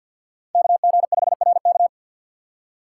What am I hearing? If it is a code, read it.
Morse code, heard as a keyed tone, about 52 wpm, 700 Hz. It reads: KG5RK
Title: USA Call Signs - 2x Character Spacing - 50wpm